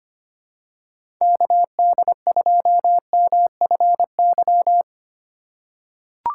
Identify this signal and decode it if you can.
Morse code, heard as a keyed tone, about 25 wpm, 700 Hz. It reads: KD2MFY
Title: USA Call Signs - 6 Characters - 2x Character Spacing - 25wpm